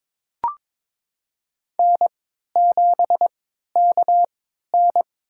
Morse code, heard as a keyed tone, about 22 wpm, spaced as for 13 wpm, 700 Hz. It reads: N7KN
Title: USA Call Signs - 3x Character Spacing - 22wpm